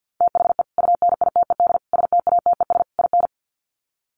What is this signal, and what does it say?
Morse code, heard as a keyed tone, about 48 wpm, 700 Hz. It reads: THE UNITED STATES IN
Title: Sets of 4 Words 50wpm